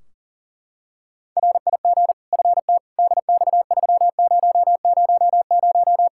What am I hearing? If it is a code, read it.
Morse code, heard as a keyed tone, about 40 wpm, 700 Hz. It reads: RIG FT DX3000
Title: K1USN SST QSO Practice 40wpm